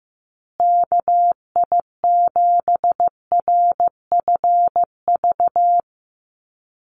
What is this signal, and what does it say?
Morse code, heard as a keyed tone, about 15 wpm, 700 Hz. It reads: KI7RFV
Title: USA Call Signs Repeated 2x 15wpm